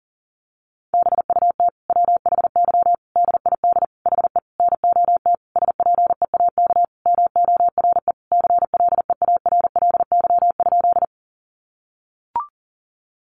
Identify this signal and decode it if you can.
Morse code, heard as a keyed tone, about 37 wpm, 700 Hz. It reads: BUT WHY DID HE NOT SPEAK MORE CLEARLY?
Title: Sentences from Top 1300 Words 40wpm